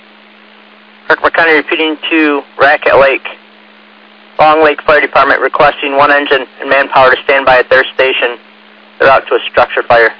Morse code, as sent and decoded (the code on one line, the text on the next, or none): none